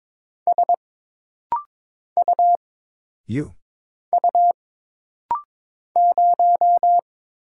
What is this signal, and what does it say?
Morse code, heard as a keyed tone, about 22 wpm, 700 Hz. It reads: S U U 0
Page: This rapid-fire timing is geared to push your ICR (Instant Character Recognition) skills.